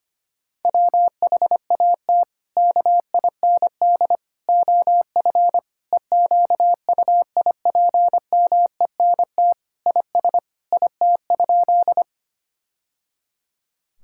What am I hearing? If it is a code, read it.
Morse code, heard as a keyed tone, about 25 wpm, 700 Hz. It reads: WHAT KIND OF EQUIPMENT IS IT?
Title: Sentences from Top 1000 Words - Review Part 5 - 25wpm